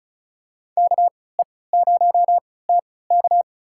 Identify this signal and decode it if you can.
Morse code, heard as a keyed tone, about 34 wpm, spaced as for 20 wpm, 700 Hz. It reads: KE0TK
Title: USA Call Signs - 5 Characters - 3x Character Spacing - 35wpm